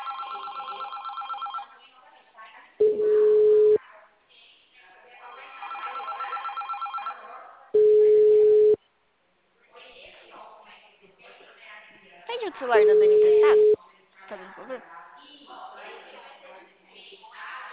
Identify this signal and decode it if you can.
Ringback tone (Europe, 425 Hz), with speech.
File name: out- -2044- - - .9483